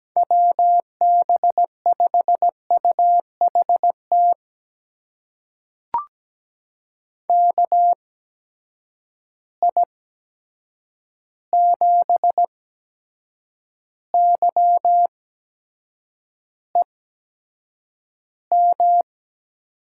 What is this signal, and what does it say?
Morse code, heard as a keyed tone, about 17 wpm, spaced as for 7 wpm, 700 Hz. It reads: WB5UHT KI7YEM
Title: USA Call Signs - 6 Characters - 8x Character Spacing - 17wpm